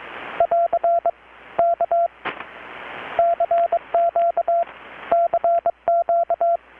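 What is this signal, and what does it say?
Morse code, heard as a keyed tone, about 23 wpm, 680 Hz, with radio noise. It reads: <AR> K CQ CQ
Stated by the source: Morse from KPH's Night of Nights XX maritime radiotelegraphy event, received in Chiapas, Mexico about 3,800 km away on a Kenwood R-600 with a grounded 10 m long-wire antenna.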